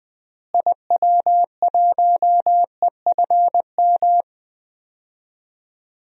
Morse code, heard as a keyed tone, about 20 wpm, 700 Hz. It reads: IW1EFM
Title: DXCC-100 Call Signs - 6 Characters - Repeated 2x - 20wpm